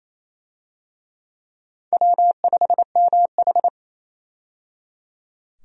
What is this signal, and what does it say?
Morse code, heard as a keyed tone, about 28 wpm, 700 Hz. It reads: W5MH
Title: USA Call Signs - 4 Characters - 28wpm